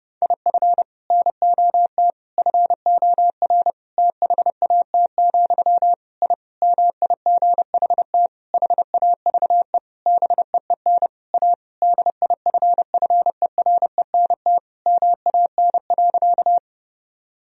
Morse code, heard as a keyed tone, about 30 wpm, 700 Hz. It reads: IF NOT FOR THAT, I MIGHT HAVE BEEN A DIFFERENT MAN.
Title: Sentences from Top 1000 Words - Review Part 3 - 30wpm